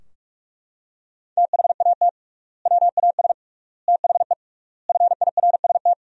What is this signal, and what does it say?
Morse code, heard as a keyed tone, about 41 wpm, 700 Hz. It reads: THAT WAS THE FIRST
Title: 3x Word-Spacing - Sentences from Top 300 Words - 45wpm